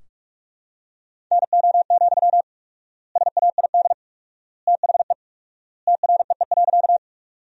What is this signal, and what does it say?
Morse code, heard as a keyed tone, about 46 wpm, 700 Hz. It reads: NO, SAID THE TREE.
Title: 4x Word-Spacing - Sentences from Top 300 Words - 45wpm